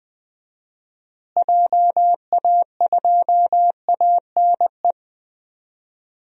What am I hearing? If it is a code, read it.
Morse code, heard as a keyed tone, about 21 wpm, 700 Hz. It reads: JA2ANE
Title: DXCC-100 Call Signs - 6 Characters - Repeated 3x - 20wpm